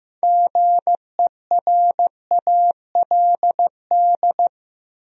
Morse code, heard as a keyed tone, about 15 wpm, 700 Hz. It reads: GERALD
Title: USA Common Names - 15wpm